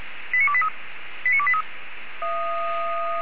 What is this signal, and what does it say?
Five-tone selective call (ZVEI), heard as quick runs of five tones, tones of about 70 ms. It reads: 89283 89283
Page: SDR/ZVEI